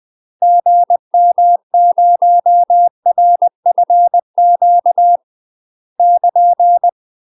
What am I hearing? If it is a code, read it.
Morse code, heard as a keyed tone, about 20 wpm, 700 Hz. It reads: GM0RFQ <KN>
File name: UK MASTER QSOs TRAINING / 20WPM / UK MASTER QSOs 0023